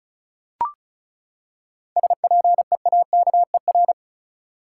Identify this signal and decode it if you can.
Morse code, heard as a keyed tone, about 34 wpm, 700 Hz. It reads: SPEAKER
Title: Seven Letter Words - 35wpm